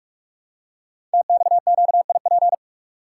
Morse code, heard as a keyed tone, about 44 wpm, 700 Hz. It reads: TXQIP
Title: Code Groups - 5 Letters - 45wpm